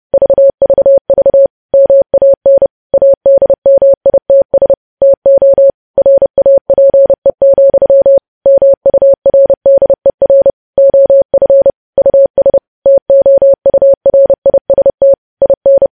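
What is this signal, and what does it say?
Morse code, heard as a keyed tone, about 30 wpm, 550 Hz. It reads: VVV MAN ADMITS TO RAPE, MURDER OF US TOURIST IN
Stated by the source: Morse news headlines sent at 30.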